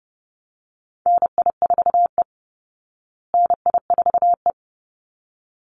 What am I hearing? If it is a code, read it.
Morse code, heard as a keyed tone, about 30 wpm, 700 Hz. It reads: NI4E NI4E
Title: Call Signs Repeated 3x - 30wpm